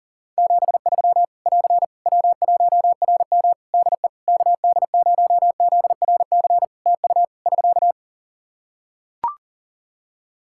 Morse code, heard as a keyed tone, about 40 wpm, 700 Hz. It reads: 73 <AR> W1RM DE KD0ZRC TU <SK>
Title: Practice QSO - Exchange 4 - Ending - 40wpm